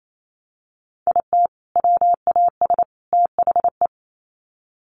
Morse code, heard as a keyed tone, about 28 wpm, 700 Hz. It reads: IT WAS THE
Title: Sets of 3 Words 28wpm